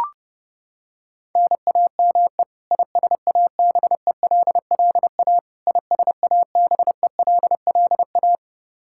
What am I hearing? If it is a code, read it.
Morse code, heard as a keyed tone, about 32 wpm, 700 Hz. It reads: NAME ISABELLA ISABELLA
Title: K1USN SST QSO Practice 30wpm